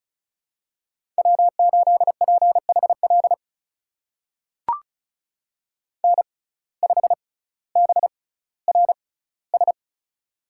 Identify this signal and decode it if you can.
Morse code, heard as a keyed tone, about 36 wpm, spaced as for 18 wpm, 700 Hz. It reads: W8PHL N5BRS
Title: USA Call Signs - 5 Characters - 6x Character Spacing - 35wpm